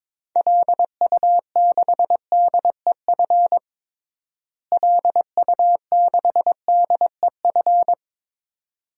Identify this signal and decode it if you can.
Morse code, heard as a keyed tone, about 22 wpm, 700 Hz. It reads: LU6DEF LU6DEF
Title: DXCC-100 Call Signs - 6 Characters - Repeated 2x - 22wpm